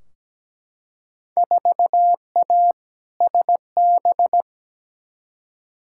Morse code, heard as a keyed tone, about 17 wpm, 700 Hz. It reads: 4A SB
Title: Field Day Exchanges Encore - 17wpm